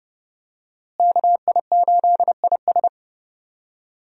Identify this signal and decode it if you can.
Morse code, heard as a keyed tone, about 30 wpm, 700 Hz. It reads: KI8IS
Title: USA Call Signs - 5 Characters - Repeated 3x - 30wpm